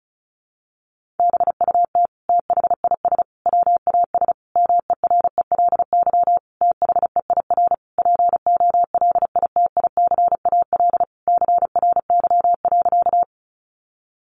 Morse code, heard as a keyed tone, about 34 wpm, 700 Hz. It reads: BUT THIS WAS MERELY THEIR POLITICAL CRY.
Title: Sentences from Top 1400 Words 35wpm